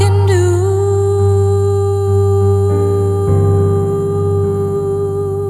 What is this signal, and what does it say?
Dial tone (Japan), 400 Hz, with noise.